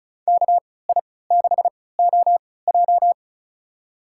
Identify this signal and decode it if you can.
Morse code, heard as a keyed tone, about 34 wpm, spaced as for 20 wpm, 700 Hz. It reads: KI6OJ
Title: USA Call Signs - 5 Characters - 3x Character Spacing - 35wpm